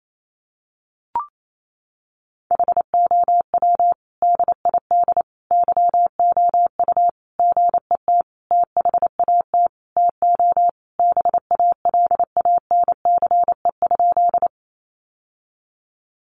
Morse code, heard as a keyed tone, about 28 wpm, 700 Hz. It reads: HOW DID YOU GET THAT TO BALANCE?
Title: Sentences from Top 900 Words 28wpm